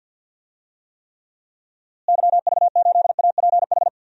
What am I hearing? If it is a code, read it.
Morse code, heard as a keyed tone, about 50 wpm, 700 Hz. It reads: YV8APH
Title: DXCC-100 Call Signs - 6 Characters - Repeated 3x - 50wpm